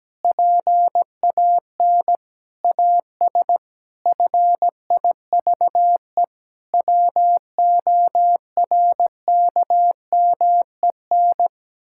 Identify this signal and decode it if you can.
Morse code, heard as a keyed tone, about 17 wpm, 700 Hz. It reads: PAN AS FIVE WORKMEN
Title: Pangram Challenge v2 - 17wpm